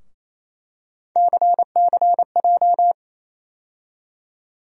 Morse code, heard as a keyed tone, about 28 wpm, 700 Hz. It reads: CCJ
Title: USA Call Sign Postfixes - 3 Letters 28wpm